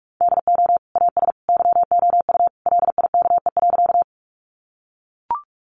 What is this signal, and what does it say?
Morse code, heard as a keyed tone, about 44 wpm, 700 Hz. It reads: DO AS YOU LIKE.
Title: Sentences from Top 100 Words 45wpm